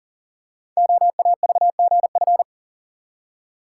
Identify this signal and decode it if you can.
Morse code, heard as a keyed tone, about 37 wpm, 690 Hz. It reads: OAVGF
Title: Code Groups - 5 Letters - 40wpm